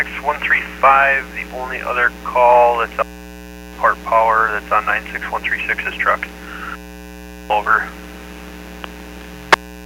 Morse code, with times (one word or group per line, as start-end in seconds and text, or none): none